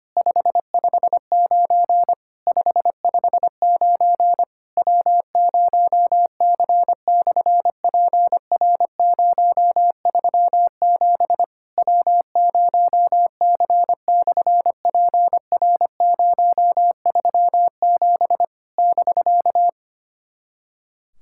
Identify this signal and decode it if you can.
Morse code, heard as a keyed tone, about 25 wpm, 700 Hz. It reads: 559 559 W0C/PR037 W0C/PR037 <BK>
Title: SOTA QSO Practice 25wpm